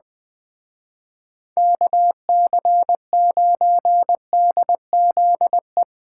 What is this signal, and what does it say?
Morse code, heard as a keyed tone, about 23 wpm, 700 Hz. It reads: KC9DZE
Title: Speed Racing - USA Call Signs 20wpm